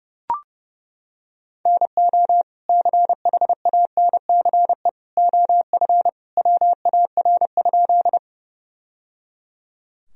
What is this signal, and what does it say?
Morse code, heard as a keyed tone, about 30 wpm, 700 Hz. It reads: NO CHANCE OF WAR?
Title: Sentences from Top 1000 Words - Review Part 2 - 30wpm